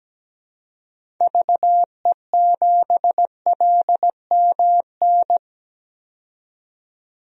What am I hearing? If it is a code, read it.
Morse code, heard as a keyed tone, about 17 wpm, 700 Hz. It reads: VE7LMN
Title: DXCC-100 Call Signs - 6 Characters - 17wpm